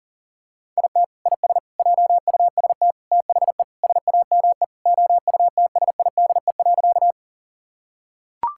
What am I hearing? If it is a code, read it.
Morse code, heard as a keyed tone, about 40 wpm, 700 Hz. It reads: IT IS JUST THE SAME OUTSIDE.
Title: Sentences from Top 1000 Words - Review Part 6 - 40wpm